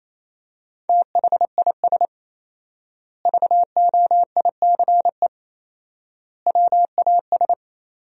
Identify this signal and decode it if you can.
Morse code, heard as a keyed tone, about 28 wpm, 700 Hz. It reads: THIS VOICE WAS
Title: 4x Word-Spacing - Sentences from Top 500 Words - 28wpm